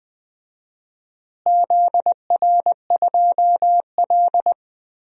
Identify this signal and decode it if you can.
Morse code, heard as a keyed tone, about 20 wpm, 700 Hz. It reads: ZR2L
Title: DXCC-100 Call Signs - 4 Characters - Repeated 3x - 20wpm